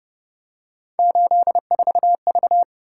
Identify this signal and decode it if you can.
Morse code, heard as a keyed tone, about 28 wpm, 700 Hz. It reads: 84V
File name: Speed-Racing - Three Letter-Number Combinations - 25wpm